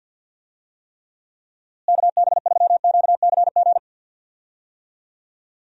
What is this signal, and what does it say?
Morse code, heard as a keyed tone, about 46 wpm, 700 Hz. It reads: KB3QCG
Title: USA Call Signs - 6 Characters - Repeated 3x - 50wpm